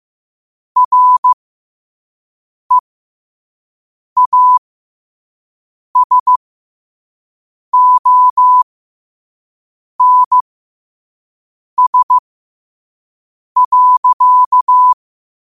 Morse code, heard as a keyed tone, about 15 wpm, spaced as for 5 wpm, 1000 Hz. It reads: REASONS.